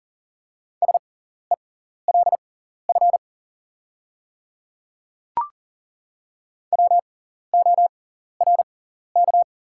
Morse code, heard as a keyed tone, about 40 wpm, spaced as for 14 wpm, 690 Hz. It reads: SELF WORK
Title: 6x Character Spacing - Top 300 Words - 40wpm